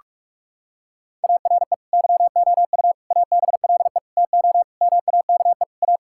Morse code, heard as a keyed tone, about 48 wpm, 700 Hz. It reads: ARE YOU ABLE TO MAKE A
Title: Sentences from Top 1000 Words - Review Part 6 - 45wpm